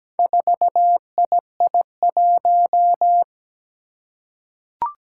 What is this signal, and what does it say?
Morse code, heard as a keyed tone, about 17 wpm, 700 Hz. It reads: 4II1
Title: Four Challenging Letter-Number Combinations 17wpm